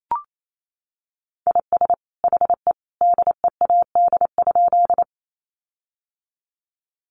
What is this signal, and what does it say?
Morse code, heard as a keyed tone, about 28 wpm, 700 Hz. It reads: IS HE DEAD?
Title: Sentences from Top 1500 Words 28wpm